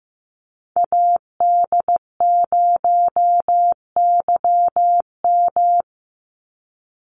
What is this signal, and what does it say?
Morse code, heard as a keyed tone, about 15 wpm, 700 Hz. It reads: AD0YM
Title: NA SOTA Chaser Callsigns 15wpm